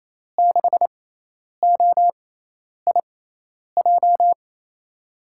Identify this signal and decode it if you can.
Morse code, heard as a keyed tone, about 28 wpm, spaced as for 10 wpm, 700 Hz. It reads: 6OIJ
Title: USA Call Signs - 6 Characters - 6x Character Spacing - 28wpm